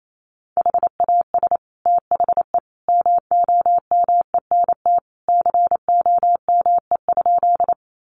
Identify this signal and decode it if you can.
Morse code, heard as a keyed tone, about 28 wpm, 700 Hz. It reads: HAS THE MOMENT COME?
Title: Sentences from Top 1000 Words 28wpm